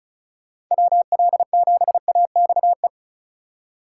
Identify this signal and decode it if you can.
Morse code, heard as a keyed tone, about 45 wpm, 700 Hz. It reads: WL7AXE